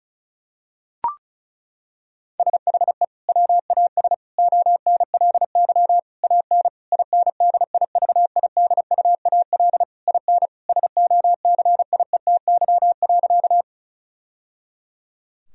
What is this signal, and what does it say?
Morse code, heard as a keyed tone, about 36 wpm, 700 Hz. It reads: SHE WAS ONLY AN INDIVIDUAL IN SOCIETY.
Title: Sentences from Top 1000 Words - Review Part 4 - 35wpm